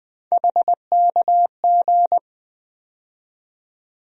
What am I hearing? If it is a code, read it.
Morse code, heard as a keyed tone, about 20 wpm, 700 Hz. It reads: HKG
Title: USA Call Sign Postfixes - 3 Letters 20wpm